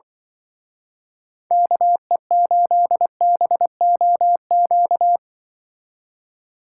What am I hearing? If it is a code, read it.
Morse code, heard as a keyed tone, about 23 wpm, 700 Hz. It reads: KE8BOQ
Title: Speed Racing - USA 2x3 Call Signs 20wpm